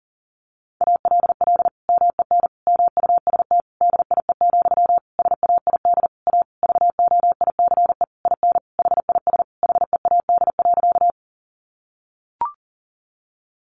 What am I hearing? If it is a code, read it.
Morse code, heard as a keyed tone, about 40 wpm, 700 Hz. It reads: ALL MEN MUST DIE, SAID A VOICE IN HIS HEAD.
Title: Sentences from Top 1300 Words 40wpm